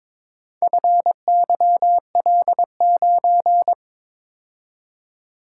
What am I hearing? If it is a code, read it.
Morse code, heard as a keyed tone, about 22 wpm, 700 Hz. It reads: FYL9
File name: practice / 22wpm / Four Letter-Number Combinations - 22wpm